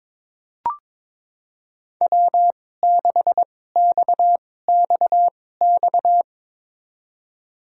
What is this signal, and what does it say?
Morse code, heard as a keyed tone, about 22 wpm, spaced as for 16 wpm, 700 Hz. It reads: W6XXX
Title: USA Call Signs - 5 Characters - 2x Character Spacing - 22wpm